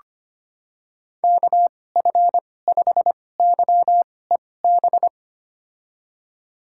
Morse code, heard as a keyed tone, about 25 wpm, spaced as for 18 wpm, 700 Hz. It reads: KF5YEB
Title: USA Call Signs - 6 Characters - 2x Character Spacing - 25wpm